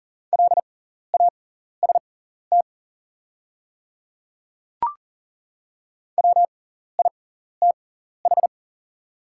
Morse code, heard as a keyed tone, about 40 wpm, spaced as for 14 wpm, 700 Hz. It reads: LAST WITH
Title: 6x Character Spacing - Top 300 Words - 40wpm